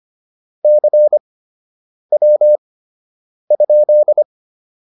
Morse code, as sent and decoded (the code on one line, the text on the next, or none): -.-. .-- ..--..
CW?